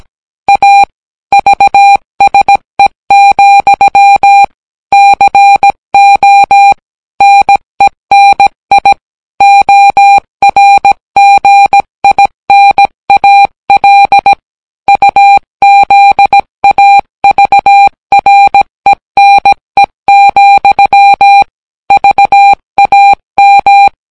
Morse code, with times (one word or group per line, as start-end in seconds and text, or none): 0.5-0.8 A
1.3-4.4 VSE,
4.9-6.7 CO
7.2-8.9 NENI
9.4-14.3 ORGINAL
14.9-21.4 UZAVRENE,
21.9-23.9 VAM